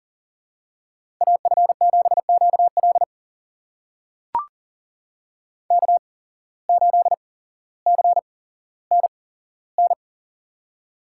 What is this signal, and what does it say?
Morse code, heard as a keyed tone, about 43 wpm, spaced as for 17 wpm, 700 Hz. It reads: AF7QL K8CNN
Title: USA Call Signs - 5 Characters - 8x Character Spacing - 40wpm